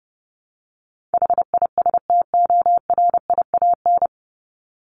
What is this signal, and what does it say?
Morse code, heard as a keyed tone, about 28 wpm, 700 Hz. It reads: HISTORIAN
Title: Top 1751-2000 Words 30wpm